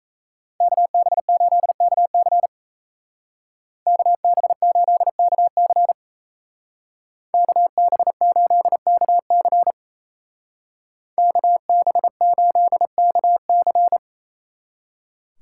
Morse code, heard as a keyed tone, about 37 wpm, 700 Hz. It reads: KB8KC KB8KC KB8KC KB8KC